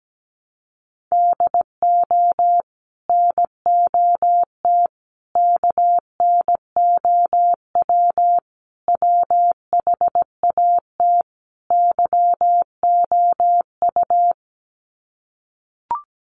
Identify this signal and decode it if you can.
Morse code, heard as a keyed tone, about 17 wpm, 700 Hz. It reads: DO NOT KNOW WHAT YOU
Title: Sets of 5 Words 17wpm